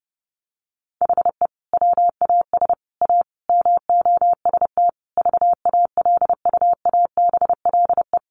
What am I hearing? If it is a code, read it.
Morse code, heard as a keyed tone, about 30 wpm, 700 Hz. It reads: HE WAS A MOST VALUABLE
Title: Sentences from Top 1200 Words 30wpm